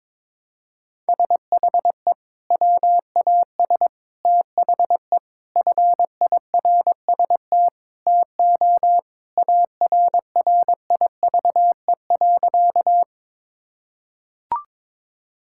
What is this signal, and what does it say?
Morse code, heard as a keyed tone, about 22 wpm, 700 Hz. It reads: SHE WAS THE FIRST TO ARRIVE.
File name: practice / 22wpm / Sentences with 3-Word Phrases - 22wpm